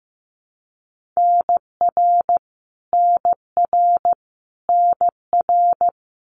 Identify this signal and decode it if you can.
Morse code, heard as a keyed tone, about 15 wpm, 700 Hz. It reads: NR NR NR
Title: Abbreviations Repeated 3x - 15wpm